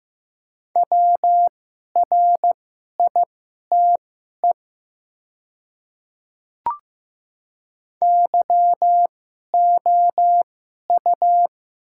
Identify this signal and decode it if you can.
Morse code, heard as a keyed tone, about 15 wpm, spaced as for 11 wpm, 700 Hz. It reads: WRITE YOU
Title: 2x Character Spacing - Top 100 Words - 15wpm